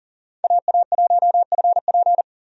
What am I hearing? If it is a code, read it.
Morse code, heard as a keyed tone, about 40 wpm, 700 Hz. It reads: AA1FP
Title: USA Call Signs - 5 Characters - 40wpm